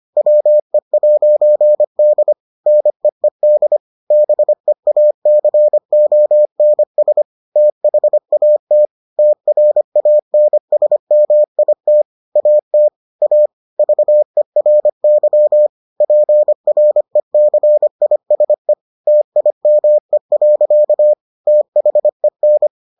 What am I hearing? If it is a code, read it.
Morse code, heard as a keyed tone, about 25 wpm, 600 Hz. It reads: WE'D NEED BEACONS THAT TRANSMIT AT A VERY PRECISE TIME. THEN